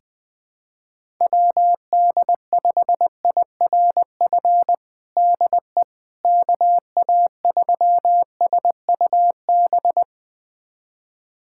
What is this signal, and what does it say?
Morse code, heard as a keyed tone, about 20 wpm, 700 Hz. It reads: WD5IRF DE KA3SUB
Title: USA Callsign DE Callsign - 6 Characters - 20wpm